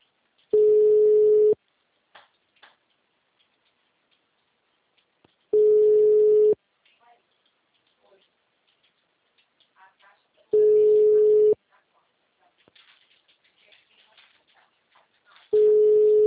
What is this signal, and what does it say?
Ringback tone (Europe, 425 Hz), with speech.